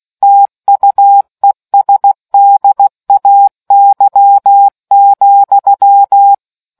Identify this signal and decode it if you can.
Morse code, heard as a keyed tone, about 16 wpm, 800 Hz. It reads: TUESDAY,